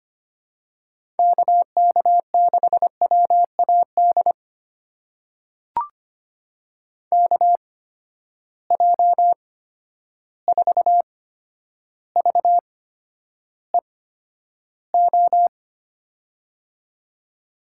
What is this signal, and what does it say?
Morse code, heard as a keyed tone, about 25 wpm, spaced as for 11 wpm, 700 Hz. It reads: KK6WAD KJ4VEO
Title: USA Call Signs - 6 Characters - 8x Character Spacing - 25wpm